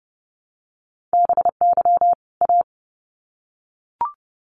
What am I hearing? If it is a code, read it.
Morse code, heard as a keyed tone, about 30 wpm, 700 Hz. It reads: BY A
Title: Sets of 2 Words 30wpm